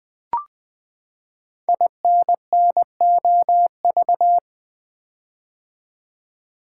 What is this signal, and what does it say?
Morse code, heard as a keyed tone, about 20 wpm, 700 Hz. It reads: INNOV
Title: Build A Word - 5 Letters At A Time - 10-Letter Words - 20wpm